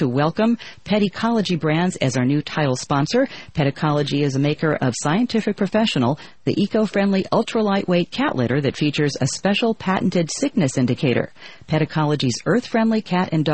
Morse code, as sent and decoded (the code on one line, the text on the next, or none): none